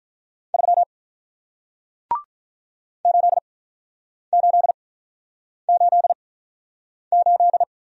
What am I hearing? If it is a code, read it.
Morse code, heard as a keyed tone, about 46 wpm, spaced as for 9 wpm, 700 Hz. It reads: 3 8888